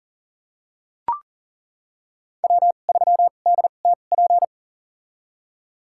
Morse code, heard as a keyed tone, about 38 wpm, spaced as for 28 wpm, 700 Hz. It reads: W3DTP
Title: USA Call Signs - 5 Characters - 2x Character Spacing - 40wpm